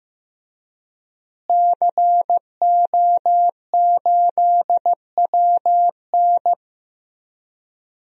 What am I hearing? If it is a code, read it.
Morse code, heard as a keyed tone, about 15 wpm, 700 Hz. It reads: CO8WN
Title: DXCC-100 Call Signs - 5 Characters - Repeated 2x - 15wpm